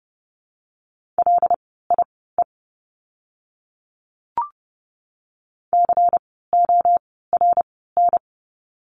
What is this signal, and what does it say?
Morse code, heard as a keyed tone, about 30 wpm, spaced as for 17 wpm, 700 Hz. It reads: LIE CORN
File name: Lesson 16f - Letter c - Words 30wpm